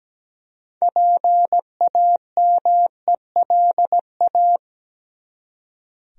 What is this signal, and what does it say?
Morse code, heard as a keyed tone, about 17 wpm, 700 Hz. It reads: PAMELA
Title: USA Common Names - 17wpm